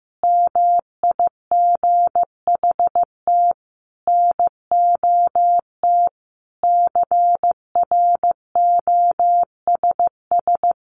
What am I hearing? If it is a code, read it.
Morse code, heard as a keyed tone, about 15 wpm, 700 Hz. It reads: MIGHT NOT CROSS
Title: A Sound Of Thunder 15wpm